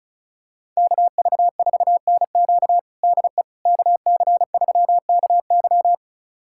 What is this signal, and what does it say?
Morse code, heard as a keyed tone, about 34 wpm, 700 Hz. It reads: KV4NQ DE KC3KY
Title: USA Callsign DE Callsign 35wpm